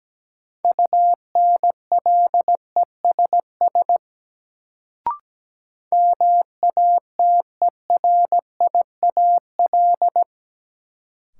Morse code, heard as a keyed tone, about 17 wpm, 700 Hz. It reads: UNLESS MATERIAL
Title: Sentences from Top 1000 Words - Review Part 5 - New Word Drill - 17wpm